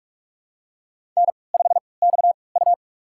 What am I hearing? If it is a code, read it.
Morse code, heard as a keyed tone, about 47 wpm, spaced as for 26 wpm, 700 Hz. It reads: N5XU
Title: USA Call Signs - 4 Characters - 3x Character Spacing - 45wpm